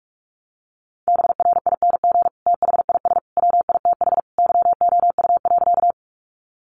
Morse code, heard as a keyed tone, about 44 wpm, 700 Hz. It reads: BRING THIS WITH YOU.
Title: Sentences from Top 500 Words 45wpm